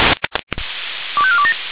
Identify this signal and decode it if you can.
Five-tone selective call (ZVEI), heard as quick runs of five tones, tones of about 70 ms. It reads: 25427
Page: /496/SDR/ZVEI/